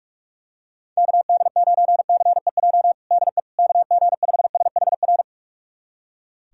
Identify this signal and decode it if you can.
Morse code, heard as a keyed tone, about 41 wpm, 700 Hz. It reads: KD9CEJ DE KG5SHR
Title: USA Callsign DE Callsign - 6 Characters - 45wpm